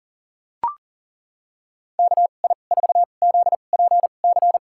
Morse code, heard as a keyed tone, about 37 wpm, spaced as for 28 wpm, 700 Hz. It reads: KI4ZPC
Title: USA Call Signs - 2x Character Spacing - 40wpm